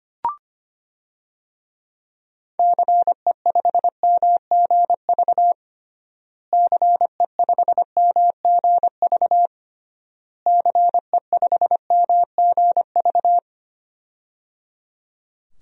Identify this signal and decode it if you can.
Morse code, heard as a keyed tone, about 25 wpm, 700 Hz. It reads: CE5MGV CE5MGV CE5MGV
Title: DXCC-100 Call Signs - 6 Characters - Repeated 3x - 25wpm